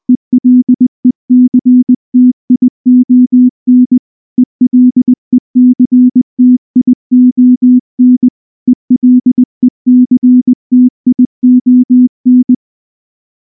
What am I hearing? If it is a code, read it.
Morse code, heard as a keyed tone, about 20 wpm, 250 Hz. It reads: ELECTION ELECTION ELECTION
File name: morse / morse-training-20WPM